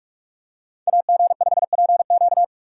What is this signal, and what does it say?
Morse code, heard as a keyed tone, about 40 wpm, 700 Hz. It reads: AG5PQ